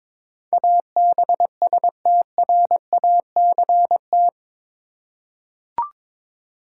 Morse code, heard as a keyed tone, about 22 wpm, 700 Hz. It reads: ABSTRACT
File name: practice / 22wpm / Eight Letter Words - 22wpm